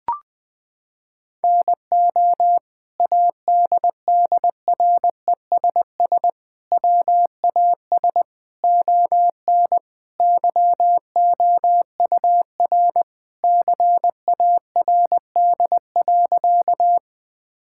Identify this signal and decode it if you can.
Morse code, heard as a keyed tone, about 20 wpm, 700 Hz. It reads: NO ADDRESS WAS ON YOUR CARD.
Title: Sentences from Top 1000 Words - Review Part 6 - 20wpm